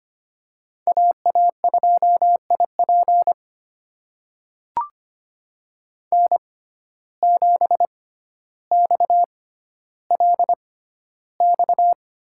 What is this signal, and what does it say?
Morse code, heard as a keyed tone, about 25 wpm, spaced as for 13 wpm, 700 Hz. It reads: AA2IP N7XLX
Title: USA Call Signs - 5 Characters - 6x Character Spacing - 25wpm